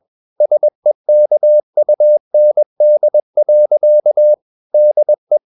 Morse code, heard as a keyed tone, about 21 wpm, 600 Hz. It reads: SEKUND. DE
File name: Ferie QRS 21wpm